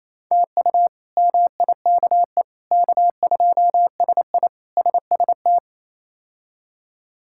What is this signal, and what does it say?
Morse code, heard as a keyed tone, about 28 wpm, 700 Hz. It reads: TU MIKE K2SI SST
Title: K1USN SST QSO Practice - 28wpm